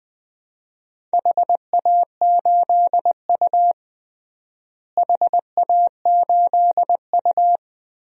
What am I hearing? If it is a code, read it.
Morse code, heard as a keyed tone, about 20 wpm, 700 Hz. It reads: HA8U HA8U
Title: DXCC-100 Call Signs - 4 Characters - Repeated 3x - 20wpm